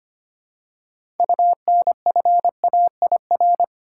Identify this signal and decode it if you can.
Morse code, heard as a keyed tone, about 25 wpm, 700 Hz. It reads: UNFAIR
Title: Six Letter Words - 25wpm